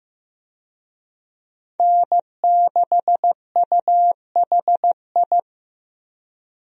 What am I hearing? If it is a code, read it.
Morse code, heard as a keyed tone, about 15 wpm, 700 Hz. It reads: N6UHI
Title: USA Call Signs - 5 Characters - 15wpm